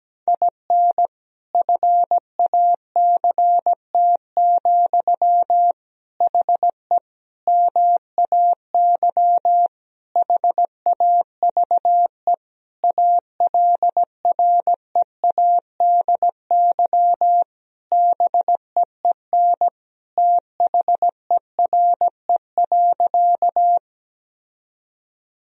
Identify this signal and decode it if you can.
Morse code, heard as a keyed tone, about 17 wpm, 700 Hz. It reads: IN FACT, HE MAY HAVE ALREADY BEEN THERE.
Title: Sentences from Top 1000 Words - Review Part 3 - 17wpm